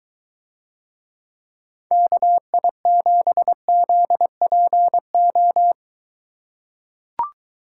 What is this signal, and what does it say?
Morse code, heard as a keyed tone, about 23 wpm, 700 Hz. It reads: KI7ZPO